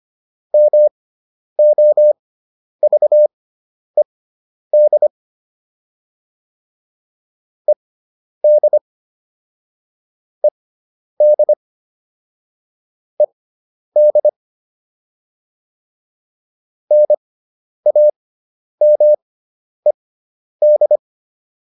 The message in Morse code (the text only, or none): MOVED ED ED ED NAMED